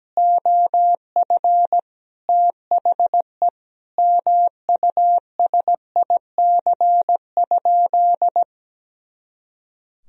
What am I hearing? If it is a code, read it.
Morse code, heard as a keyed tone, about 17 wpm, 700 Hz. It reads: OF THE MUSIC?
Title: Sentences from Top 300 Words and Abbreviations 17wpm